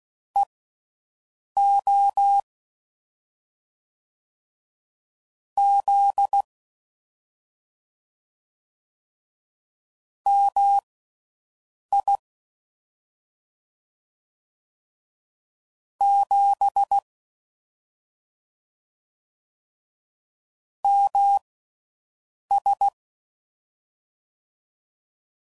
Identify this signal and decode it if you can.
Morse code, heard as a keyed tone, about 16 wpm, spaced as for 6 wpm, 800 Hz. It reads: EO Z MI 7 MS
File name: Mixed / CW Comparisons / Comparisons Broken Characters